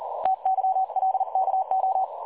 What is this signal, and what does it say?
Morse code, heard as a keyed tone, about 38 wpm, 760 Hz. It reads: TYLKO
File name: dlazaawansowanych / nagraniazpasma / websdr recording 3565.0kHz